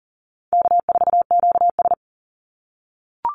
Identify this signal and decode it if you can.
Morse code, heard as a keyed tone, about 40 wpm, 700 Hz. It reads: K4QS
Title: NA SOTA Chaser Callsigns 40wpm